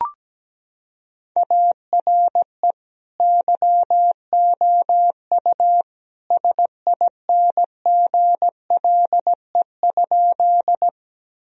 Morse code, heard as a keyed tone, about 17 wpm, 700 Hz. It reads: ARE YOU SINGLE?
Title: Sentences from Top 1000 Words - Review Part 4 - 17wpm